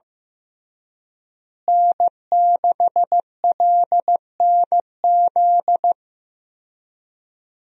Morse code, heard as a keyed tone, about 15 wpm, 700 Hz. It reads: N6LNZ
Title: USA Call Signs - 5 Characters - Repeated 2x - 15wpm